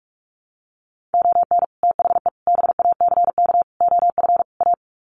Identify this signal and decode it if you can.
Morse code, heard as a keyed tone, about 44 wpm, 700 Hz. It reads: ON THE BACK OF A
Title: Sets of 5 Words Encore2 45wpm